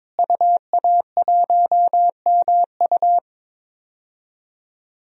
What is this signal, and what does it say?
Morse code, heard as a keyed tone, about 22 wpm, 700 Hz. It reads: UA1MU
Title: DXCC-100 Call Signs - 5 Characters - Repeated 3x - 22wpm